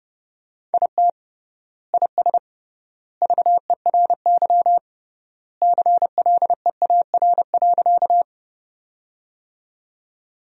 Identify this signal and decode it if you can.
Morse code, heard as a keyed tone, about 30 wpm, 700 Hz. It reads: IT IS VERY CLEAR.
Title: 3x Word-Spacing - Sentences from Top 500 Words - 30wpm